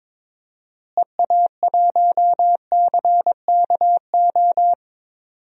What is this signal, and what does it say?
Morse code, heard as a keyed tone, about 22 wpm, 700 Hz. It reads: EA1CKO
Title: DXCC-100 Call Signs - 6 Characters - Repeated 2x - 22wpm